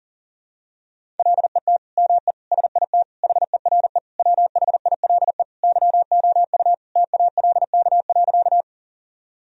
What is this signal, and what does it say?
Morse code, heard as a keyed tone, about 40 wpm, 700 Hz. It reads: LET ME SIT HERE WHILE YOU TALK.
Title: Sentences from Top 1000 Words - Review Part 1 - 40wpm